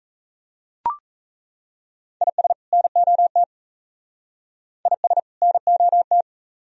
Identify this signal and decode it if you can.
Morse code, heard as a keyed tone, about 33 wpm, 700 Hz. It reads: IS NOT IS NOT
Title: Speed-Racing - Sets of 2 Words 28wpm